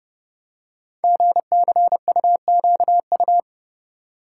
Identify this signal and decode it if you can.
Morse code, heard as a keyed tone, about 30 wpm, 700 Hz. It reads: GCUQU
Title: Code Groups - 5 Letters - 30wpm